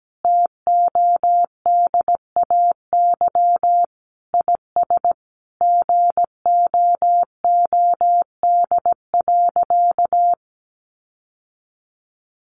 Morse code, heard as a keyed tone, about 17 wpm, 700 Hz. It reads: TODAY IS GOOD.
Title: Sentences from Top 1400 Words 17wpm